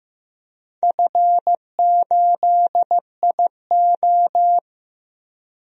Morse code, heard as a keyed tone, about 15 wpm, 700 Hz. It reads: F8IO
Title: DXCC-100 Call Signs - 4 Characters - Repeated 3x - 15wpm